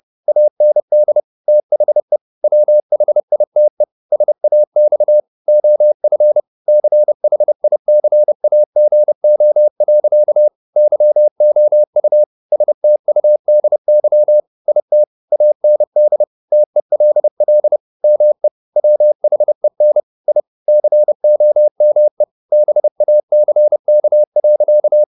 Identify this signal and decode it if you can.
Morse code, heard as a keyed tone, about 30 wpm, 600 Hz. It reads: AND THE WHITE SAX OF CHICAGO. YOU STUDY IT AND TELL ME WHEN I COME BACK.
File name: the old man and the sea / the old man and the sea 1 30wpm